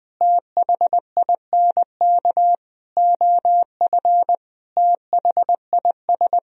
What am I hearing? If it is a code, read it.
Morse code, heard as a keyed tone, about 20 wpm, 700 Hz. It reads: THINK OF THIS
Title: Sentences from Top 1000 Words - Review Part 2 - 20wpm